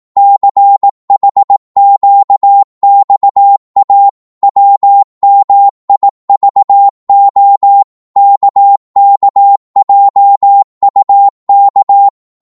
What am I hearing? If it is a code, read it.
Morse code, heard as a keyed tone, about 18 wpm, 800 Hz. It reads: CHQXA WMIVO KKJUK
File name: Morse / ltr-18wpm